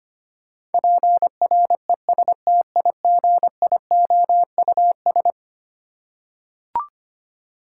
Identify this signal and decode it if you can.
Morse code, heard as a keyed tone, about 25 wpm, 700 Hz. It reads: PRESTIGIOUS
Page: morse-code-ninja-practice-25wpm